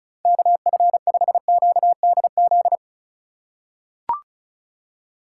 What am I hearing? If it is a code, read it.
Morse code, heard as a keyed tone, about 36 wpm, 700 Hz. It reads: KF5QDZ
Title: USA Call Signs - 6 Characters - 4x Character Spacing - 35wpm